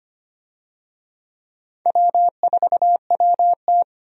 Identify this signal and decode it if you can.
Morse code, heard as a keyed tone, about 25 wpm, 700 Hz. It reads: W4WT
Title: USA Call Signs - 4 Characters - 25wpm